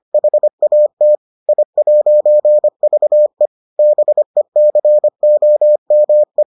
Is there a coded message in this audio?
Yes: Morse code from 0.1 to 6.4 s, heard as a keyed tone, about 25 wpm, 600 Hz. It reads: HAT I'VE BECOME